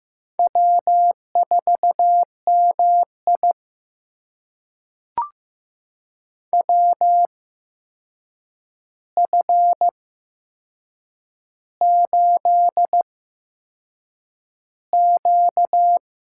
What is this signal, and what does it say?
Morse code, heard as a keyed tone, about 15 wpm, spaced as for 6 wpm, 700 Hz. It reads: W4MI WF8Q
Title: USA Call Signs - 8x Character Spacing - 15wpm